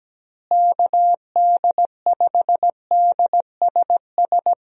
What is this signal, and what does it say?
Morse code, heard as a keyed tone, about 17 wpm, 700 Hz. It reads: KD5DSS
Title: USA Call Signs - 6 Characters - Repeated 2x - 17wpm